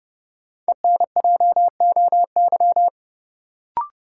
Morse code, heard as a keyed tone, about 31 wpm, 700 Hz. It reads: ENJOY
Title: Sentences from Top 1300 Words - New Word Drill Alt - 30wpm